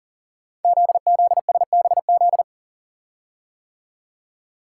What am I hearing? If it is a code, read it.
Morse code, heard as a keyed tone, about 38 wpm, 690 Hz. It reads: ZZSBZ
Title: Code Groups - 5 Letters - 40wpm